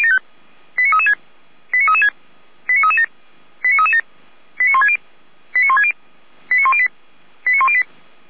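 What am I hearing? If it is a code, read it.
Five-tone selective call (ZVEI), heard as quick runs of five tones, tones of about 70 ms. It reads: 89336 89336 89337 89337 89160 89160 89108 89108
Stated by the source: ZVEI five-tone call received by SDR.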